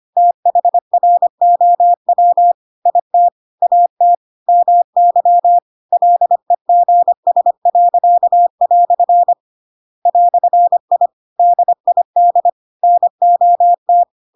Morse code, heard as a keyed tone, about 25 wpm, 700 Hz. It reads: THROW IT AT MY LEGS." "I DID NOT
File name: Pinocchio / Pinocchio ch2 25 25 ik1hge